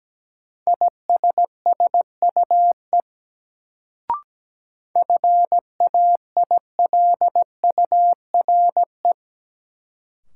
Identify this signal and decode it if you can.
Morse code, heard as a keyed tone, about 17 wpm, 700 Hz. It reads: ISSUE FAILURE
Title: Sentences from Top 900 Words - New Word Drill - 17wpm